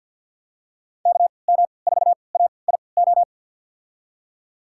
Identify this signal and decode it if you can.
Morse code, heard as a keyed tone, about 52 wpm, spaced as for 29 wpm, 700 Hz. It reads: KM4AIO
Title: USA Call Signs - 6 Characters - 3x Character Spacing - 50wpm